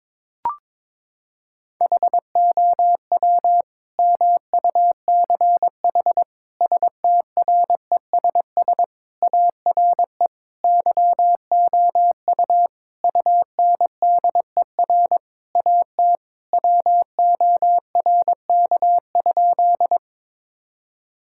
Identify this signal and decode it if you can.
Morse code, heard as a keyed tone, about 22 wpm, 700 Hz. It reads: HOW MUCH STRESS ARE YOU UNDER AT WORK?
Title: Sentences from Top 1000 Words - Review Part 6 - 22wpm